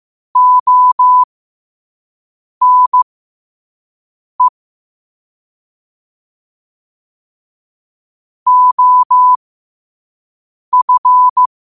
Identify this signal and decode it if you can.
Morse code, heard as a keyed tone, about 15 wpm, spaced as for 5 wpm, 1000 Hz. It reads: ONE OF
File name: qotdfiles / 5wpm